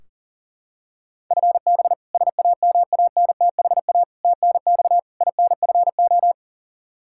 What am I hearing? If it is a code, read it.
Morse code, heard as a keyed tone, about 40 wpm, 700 Hz. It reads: FB SAMANTHA TNX INFO
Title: K1USN SST QSO Practice 40wpm